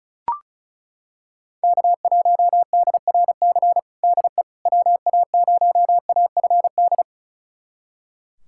Morse code, heard as a keyed tone, about 36 wpm, 700 Hz. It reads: K1DRC DE WA0AFD
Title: USA Callsign DE Callsign 35wpm